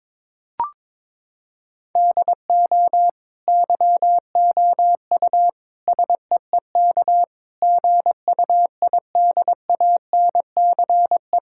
Morse code, heard as a keyed tone, about 22 wpm, 700 Hz. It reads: DO YOU SEEK GUIDANCE
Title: Sentences from Top 1600 Words 22wpm